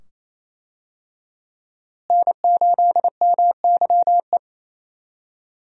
Morse code, heard as a keyed tone, about 28 wpm, 700 Hz. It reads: N8MYE
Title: USA Call Signs - 5 Characters - Repeated 2x - 28wpm